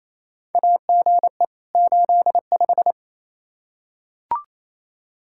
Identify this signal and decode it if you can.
Morse code, heard as a keyed tone, about 28 wpm, 700 Hz. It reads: AGE 85
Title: Practice QSO - Exchange 3 - More Info - 28wpm